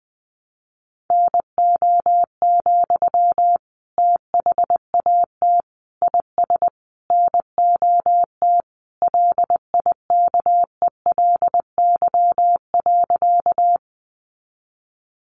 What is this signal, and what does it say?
Morse code, heard as a keyed tone, about 20 wpm, 700 Hz. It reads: NO, THAT IS NOT LIKELY.
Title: Sentences from Top 600 Words 20wpm